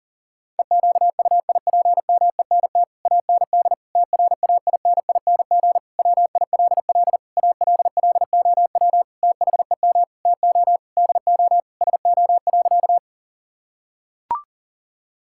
Morse code, heard as a keyed tone, about 40 wpm, 700 Hz. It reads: EQUIPMENT AND TRAINING WILL ALLOW THEM TO DO SO.
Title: Sentences from Top 1000 Words - Review Part 5 - 40wpm